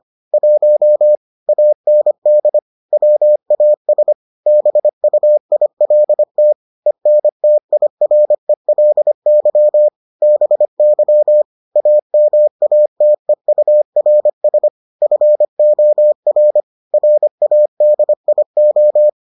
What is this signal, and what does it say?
Morse code, heard as a keyed tone, about 25 wpm, 600 Hz. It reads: 1 AND WAS BUILT ENTIRELY BY AMATEURS FOR RADIO